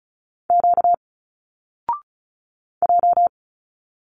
Morse code, heard as a keyed tone, about 36 wpm, 700 Hz. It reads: Q J
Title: Single Letters 35wpm